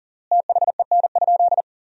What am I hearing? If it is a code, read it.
Morse code, heard as a keyed tone, about 37 wpm, 700 Hz. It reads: THEN?
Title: 3x Word-Spacing - Sentences from Top 100 Words - 40wpm